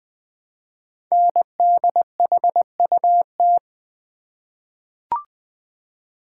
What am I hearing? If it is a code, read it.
Morse code, heard as a keyed tone, about 20 wpm, 700 Hz. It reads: NDHUT
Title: Code Groups - 5 Letters - 20wpm